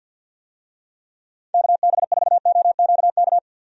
Morse code, heard as a keyed tone, about 45 wpm, 700 Hz. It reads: KB4OQK
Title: USA Call Signs - 6 Characters - Repeated 3x - 50wpm